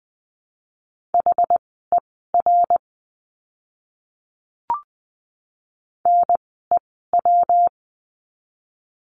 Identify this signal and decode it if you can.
Morse code, heard as a keyed tone, about 20 wpm, spaced as for 14 wpm, 700 Hz. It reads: HER NEW
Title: 115 Lesson 19f - Letter w - Words 20wpm